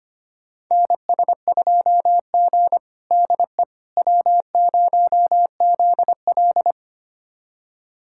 Morse code, heard as a keyed tone, about 25 wpm, 700 Hz. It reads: NS2G DE W0ZL
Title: USA Callsign DE Callsign - 4 Characters - 25wpm